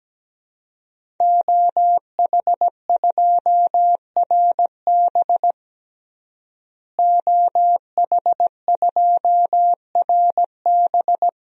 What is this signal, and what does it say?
Morse code, heard as a keyed tone, about 17 wpm, 700 Hz. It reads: OH2RB OH2RB
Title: DXCC-100 Call Signs - 5 Characters - Repeated 2x - 17wpm